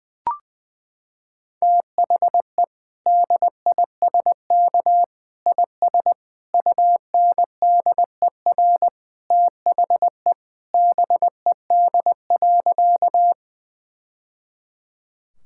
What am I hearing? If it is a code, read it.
Morse code, heard as a keyed tone, about 21 wpm, 700 Hz. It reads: THE DISK IS UNDER THE BED.
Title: Sentences from Top 1000 Words - Review Part 6 - 20wpm